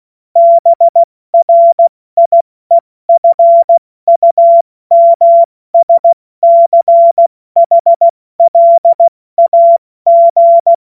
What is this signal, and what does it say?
Morse code, heard as a keyed tone, about 16 wpm, 680 Hz. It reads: BRIEFUMSCHLAG
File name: Morsekurs / Witze / Witz 06. September 2022 70 F80